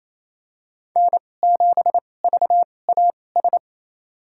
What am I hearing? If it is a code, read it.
Morse code, heard as a keyed tone, about 28 wpm, spaced as for 20 wpm, 700 Hz. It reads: N7VAS
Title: USA Call Signs - 5 Characters - 2x Character Spacing - 28wpm